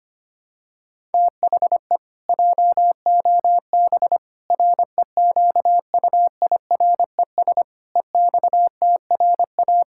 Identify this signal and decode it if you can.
Morse code, heard as a keyed tone, about 25 wpm, 700 Hz. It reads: THE JOB REQUIRES EXTRA
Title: Pangram Challenge v2 - 25wpm